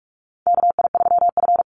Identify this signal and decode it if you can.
Morse code, heard as a keyed tone, about 45 wpm, 700 Hz. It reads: KI3F
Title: Call Signs Repeated 2x - 45wpm